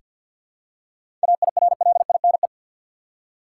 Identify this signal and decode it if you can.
Morse code, heard as a keyed tone, about 47 wpm, 700 Hz. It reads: AIRLINE
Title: Seven Letter Words - 50wpm